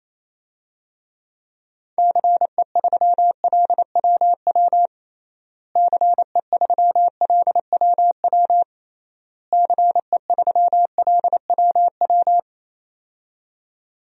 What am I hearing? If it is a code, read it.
Morse code, heard as a keyed tone, about 28 wpm, 700 Hz. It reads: CE3LWW CE3LWW CE3LWW
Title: DXCC-100 Call Signs - 6 Characters - Repeated 3x - 28wpm